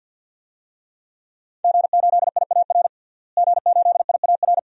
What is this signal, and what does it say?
Morse code, heard as a keyed tone, about 51 wpm, 700 Hz. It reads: G8IAR G8IAR
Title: DXCC-100 Call Signs - 5 Characters - Repeated 2x - 50wpm